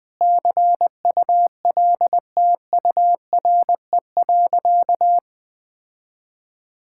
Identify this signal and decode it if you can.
Morse code, heard as a keyed tone, about 20 wpm, 700 Hz. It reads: CULTURE.
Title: Sentences from Top 1000 Words - Review Part 6 - 20wpm